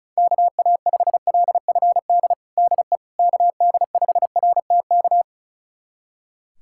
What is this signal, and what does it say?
Morse code, heard as a keyed tone, about 34 wpm, 690 Hz. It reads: KA5LFD DE KD5RTK
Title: USA Callsign DE Callsign - 6 Characters - 35wpm